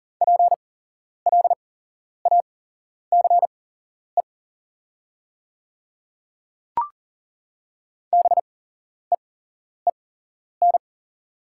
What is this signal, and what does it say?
Morse code, heard as a keyed tone, about 42 wpm, spaced as for 11 wpm, 700 Hz. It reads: PLACE BEEN
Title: 8x Character Spacing - Top 200 Words - 40wpm